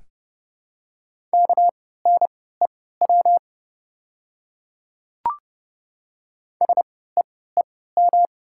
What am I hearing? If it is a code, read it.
Morse code, heard as a keyed tone, about 30 wpm, spaced as for 17 wpm, 700 Hz. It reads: KNEW SEEM
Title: 3x Character Spacing - Top 500 Words - 30wpm